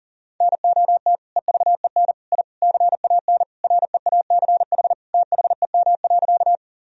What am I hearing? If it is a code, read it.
Morse code, heard as a keyed tone, about 37 wpm, 700 Hz. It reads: NOT EVEN I CAN REACH THEM.
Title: Sentences from Top 1000 Words - Review Part 2 - 40wpm